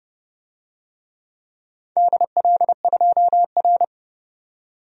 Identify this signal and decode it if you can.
Morse code, heard as a keyed tone, about 30 wpm, 700 Hz. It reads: DL2R
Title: DXCC-100 Call Signs - 4 Characters - 30wpm